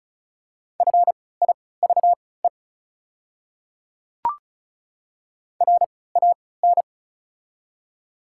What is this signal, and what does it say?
Morse code, heard as a keyed tone, about 36 wpm, spaced as for 20 wpm, 700 Hz. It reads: FIVE RAN
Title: 3x Character Spacing - Top 500 Words - 35wpm